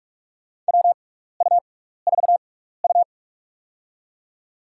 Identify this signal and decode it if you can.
Morse code, heard as a keyed tone, about 42 wpm, spaced as for 15 wpm, 700 Hz. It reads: WU4U
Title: USA Call Signs - 4 Characters - 6x Character Spacing - 45wpm